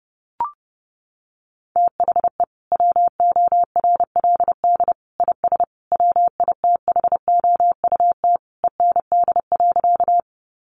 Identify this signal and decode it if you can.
Morse code, heard as a keyed tone, about 30 wpm, 700 Hz. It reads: THE WORLD IS WITHOUT END.
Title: Sentences from Top 600 Words 30wpm